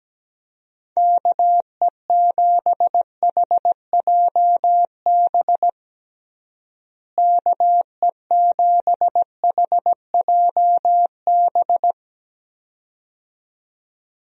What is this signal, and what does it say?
Morse code, heard as a keyed tone, about 17 wpm, 700 Hz. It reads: KE7HJB KE7HJB
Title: USA Call Signs - 6 Characters - Repeated 2x - 17wpm